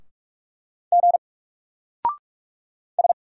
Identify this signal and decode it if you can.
Morse code, heard as a keyed tone, about 46 wpm, 700 Hz. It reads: G S
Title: Single Character (Letter:Number:Punctuation:Prosign) 45wpm